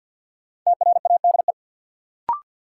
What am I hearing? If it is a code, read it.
Morse code, heard as a keyed tone, about 47 wpm, 700 Hz. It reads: TRADE
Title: Sentences from Top 600 Words - New Word Drill - 50wpm